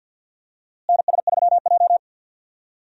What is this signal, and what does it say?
Morse code, heard as a keyed tone, about 47 wpm, 700 Hz. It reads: NS3J
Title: USA Call Signs - 4 Characters - Repeated 3x - 50wpm